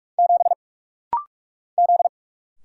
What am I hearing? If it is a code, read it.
Morse code, heard as a keyed tone, about 44 wpm, 700 Hz. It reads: 7 Z
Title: Single Letter-Number Alt - Rapid-Fire 45wpm